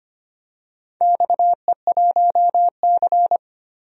Morse code, heard as a keyed tone, about 25 wpm, 700 Hz. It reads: XE1C
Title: DXCC-100 Call Signs - 4 Characters - 25wpm